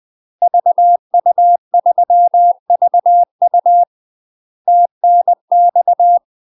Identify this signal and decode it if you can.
Morse code, heard as a keyed tone, about 20 wpm, 700 Hz. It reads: VU3VU TNX
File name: UK MASTER QSOs TRAINING / 20WPM / UK MASTER QSOs 0039